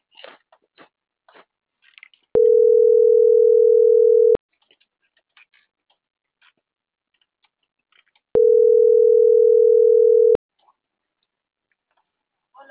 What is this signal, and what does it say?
Ringback tone (North America), 440 + 480 Hz.